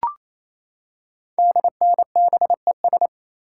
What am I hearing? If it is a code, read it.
Morse code, heard as a keyed tone, about 27 wpm, 700 Hz. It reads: DNBES
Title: Code Groups - 5 Letters - 28wpm